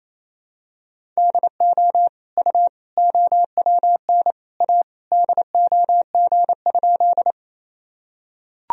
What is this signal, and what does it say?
Morse code, heard as a keyed tone, about 28 wpm, 700 Hz. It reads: DO U OWN A DOG?
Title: Sentences from Top 400 Words and Abbreviations 28wpm